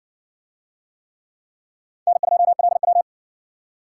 Morse code, heard as a keyed tone, about 55 wpm, 700 Hz. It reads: N2LW